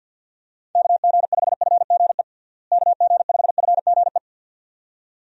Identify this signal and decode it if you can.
Morse code, heard as a keyed tone, about 52 wpm, 700 Hz. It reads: KG5FGE KG5FGE
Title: USA Call Signs - 6 Characters - Repeated 2x - 50wpm